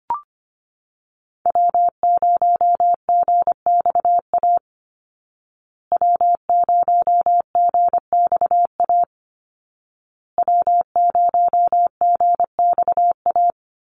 Morse code, heard as a keyed tone, about 25 wpm, 700 Hz. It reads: W0GXA W0GXA W0GXA
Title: Call Signs Repeated 2x - 25wpm